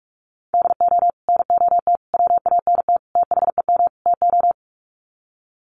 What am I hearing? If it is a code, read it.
Morse code, heard as a keyed tone, about 44 wpm, 700 Hz. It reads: DO NOT WANT THEM TO
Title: Sets of 5 Words Encore 45wpm